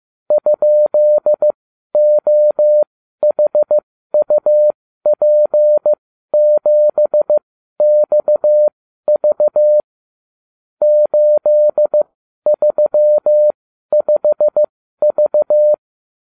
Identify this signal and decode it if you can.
Morse code, heard as a keyed tone, about 15 wpm, spaced as for 12 wpm, 600 Hz. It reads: ?OHUP7XV 835V